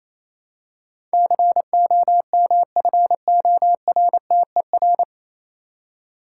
Morse code, heard as a keyed tone, about 27 wpm, 690 Hz. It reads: COMFORTER
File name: Easy Words - Speed-Racing with Hints - 22wpm